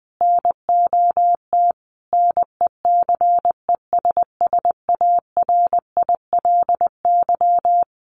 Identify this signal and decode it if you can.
Morse code, heard as a keyed tone, about 20 wpm, 700 Hz. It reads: NOT NECESSARILY
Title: Sentences from Top 1400 Words 20wpm